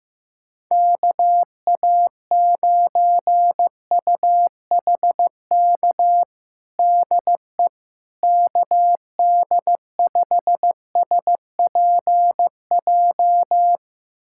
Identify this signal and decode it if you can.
Morse code, heard as a keyed tone, about 16 wpm, 700 Hz. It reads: KA9UHK DE KD5SPJ
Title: USA Callsign DE Callsign - 6 Characters - 15wpm